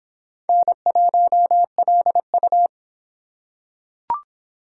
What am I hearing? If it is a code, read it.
Morse code, heard as a keyed tone, about 24 wpm, 700 Hz. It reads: N1LU